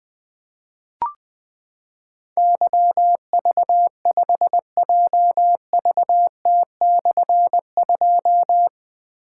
Morse code, heard as a keyed tone, about 20 wpm, 700 Hz. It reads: YV5JVT/2
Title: Difficult Call Signs 20wpm